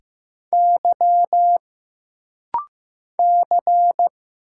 Morse code, heard as a keyed tone, about 15 wpm, 700 Hz. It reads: Y C
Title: Single Letter-Number Alt - Rapid-Fire 15wpm